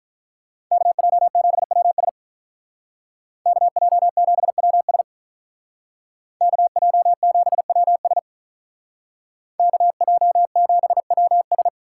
Morse code, heard as a keyed tone, about 43 wpm, 700 Hz. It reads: KJ7WS KJ7WS KJ7WS KJ7WS
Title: Speed Racing - USA Call Signs 35wpm